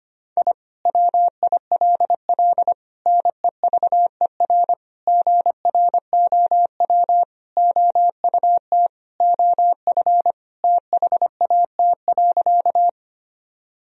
Morse code, heard as a keyed tone, about 25 wpm, 700 Hz. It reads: I WILL NEVER GROW OUT OF THAT.
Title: Sentences from Top 1000 Words - Review Part 1 - 25wpm